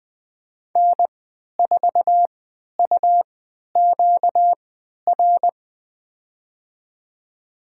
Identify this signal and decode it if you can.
Morse code, heard as a keyed tone, about 20 wpm, spaced as for 11 wpm, 700 Hz. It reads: N4UQR
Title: USA Call Signs - 3x Character Spacing - 20wpm